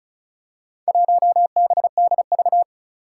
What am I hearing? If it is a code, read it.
Morse code, heard as a keyed tone, about 36 wpm, 700 Hz. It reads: 1BDV
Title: Four Challenging Letter-Number Combinations 35wpm